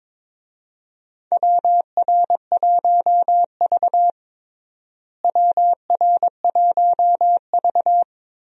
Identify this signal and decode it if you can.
Morse code, heard as a keyed tone, about 22 wpm, 700 Hz. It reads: WR1V WR1V
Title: USA Call Signs - 4 Characters - Repeated 2x - 22wpm